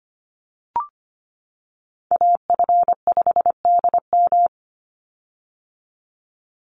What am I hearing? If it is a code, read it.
Morse code, heard as a keyed tone, about 25 wpm, 700 Hz. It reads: AF5DM
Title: Call Signs - 25wpm